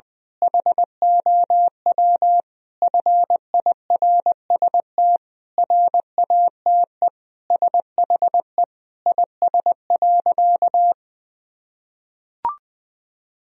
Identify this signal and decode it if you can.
Morse code, heard as a keyed tone, about 20 wpm, 700 Hz. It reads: HOW FIRST RATE SHE IS.
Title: Sentences from Top 1000 Words - Review Part 4 - 20wpm